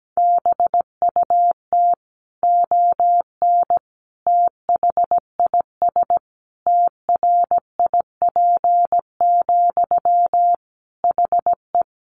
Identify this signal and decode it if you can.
Morse code, heard as a keyed tone, about 17 wpm, 700 Hz. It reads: BUT ON THIS TRIP, HE
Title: Sentences from Top 1400 Words 17wpm